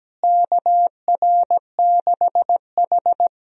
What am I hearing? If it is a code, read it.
Morse code, heard as a keyed tone, about 17 wpm, 700 Hz. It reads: KR6H
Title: USA Call Signs - 4 Characters - Repeated 2x - 17wpm